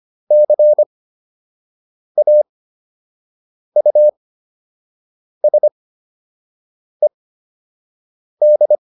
Morse code, heard as a keyed tone, about 25 wpm, spaced as for 6 wpm, 600 Hz. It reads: CAUSED